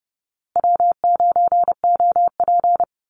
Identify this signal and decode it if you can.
Morse code, heard as a keyed tone, about 30 wpm, 700 Hz. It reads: W9OP
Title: Call Signs Repeated 2x - 30wpm